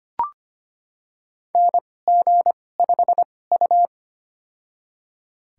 Morse code, heard as a keyed tone, about 25 wpm, spaced as for 18 wpm, 700 Hz. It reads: NG5U
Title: USA Call Signs - 4 Characters - 2x Character Spacing - 25wpm